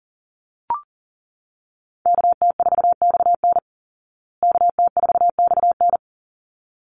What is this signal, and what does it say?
Morse code, heard as a keyed tone, about 40 wpm, 700 Hz. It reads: KT4XN KT4XN
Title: Call Signs Repeated 3x - 40wpm